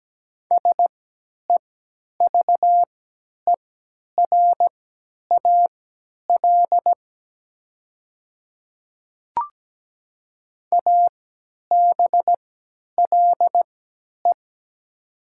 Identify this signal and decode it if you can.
Morse code, heard as a keyed tone, about 17 wpm, spaced as for 10 wpm, 700 Hz. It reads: SEVERAL ABLE
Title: 3x Character Spacing - Top 500 Words - 17wpm